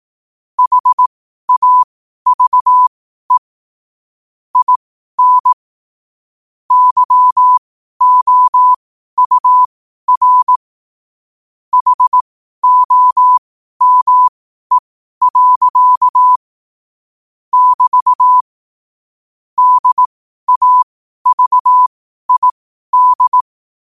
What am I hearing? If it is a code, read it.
Morse code, heard as a keyed tone, about 18 wpm, spaced as for 12 wpm, 1000 Hz. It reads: HAVE IN YOUR HOME. <BT> DAVID